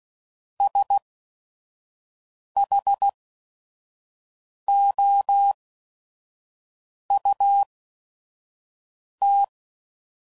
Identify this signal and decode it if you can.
Morse code, heard as a keyed tone, about 16 wpm, spaced as for 5 wpm, 800 Hz. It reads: SHOUT
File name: Prefixes / By First Letter / S vs Hs